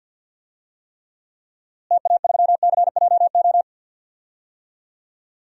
Morse code, heard as a keyed tone, about 48 wpm, 700 Hz. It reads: TA3CJO
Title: DXCC-100 Call Signs - 6 Characters - Repeated 3x - 50wpm